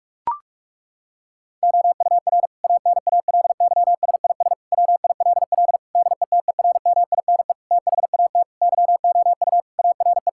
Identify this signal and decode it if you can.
Morse code, heard as a keyed tone, about 42 wpm, 700 Hz. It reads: OUR ANALYSIS WILL DETERMINE THAT YOU ARE
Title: Sentences from Top 1000 Words - Review Part 5 - 45wpm